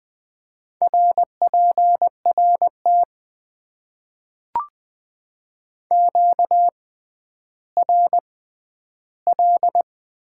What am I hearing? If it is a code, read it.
Morse code, heard as a keyed tone, about 20 wpm, spaced as for 11 wpm, 700 Hz. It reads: RPRT QRL
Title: Top 90 QSO Elements - 6x Character Spacing - 20wpm